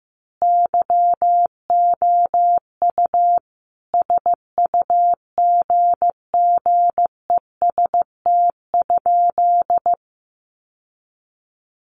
Morse code, heard as a keyed tone, about 15 wpm, 700 Hz. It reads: YOU SUGGEST?
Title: Sentences from Top 1300 Words 15wpm